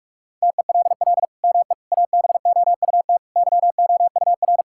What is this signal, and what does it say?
Morse code, heard as a keyed tone, about 46 wpm, 700 Hz. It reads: TELL ME ABOUT YOUR
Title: Sentences from Top 1000 Words - Review Part 6 - 45wpm